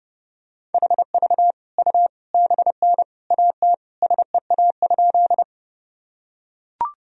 Morse code, heard as a keyed tone, about 30 wpm, 700 Hz. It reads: HV U BN AT SEA?
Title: Sentences from Top 300 Words and Abbreviations 30wpm